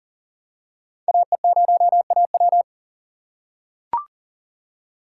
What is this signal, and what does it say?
Morse code, heard as a keyed tone, about 39 wpm, 700 Hz. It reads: AE0AW
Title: USA Call Signs - 5 Characters - 4x Character Spacing - 40wpm